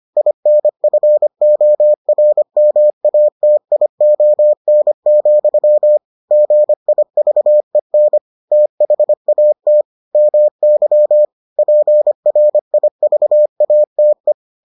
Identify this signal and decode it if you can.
Morse code, heard as a keyed tone, about 25 wpm, 600 Hz. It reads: INFORMATION, GIVEN THAT MY PRIVATE